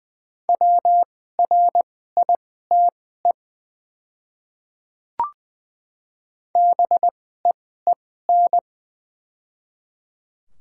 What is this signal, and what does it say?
Morse code, heard as a keyed tone, about 20 wpm, spaced as for 15 wpm, 700 Hz. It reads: WRITE BEEN
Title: 2x Character Spacing - Top 100 Words - 20wpm